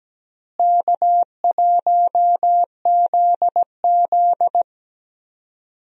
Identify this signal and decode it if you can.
Morse code, heard as a keyed tone, about 17 wpm, 700 Hz. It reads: K1ZZ
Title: USA Call Signs - 4 Characters - Repeated 2x - 17wpm